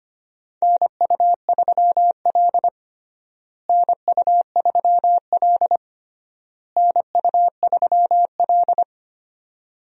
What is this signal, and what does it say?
Morse code, heard as a keyed tone, about 25 wpm, 700 Hz. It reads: NU3L NU3L NU3L
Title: USA Call Signs - 4 Characters - Repeated 3x - 25wpm